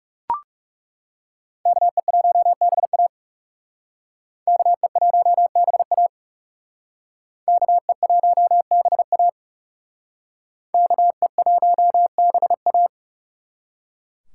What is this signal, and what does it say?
Morse code, heard as a keyed tone, about 40 wpm, 700 Hz. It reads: KE1BA KE1BA KE1BA KE1BA